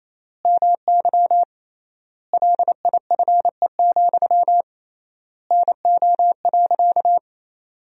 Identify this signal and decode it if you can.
Morse code, heard as a keyed tone, about 28 wpm, 700 Hz. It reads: MY LIFE, NO.
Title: 3x Word-Spacing - Sentences from Top 300 Words - 28wpm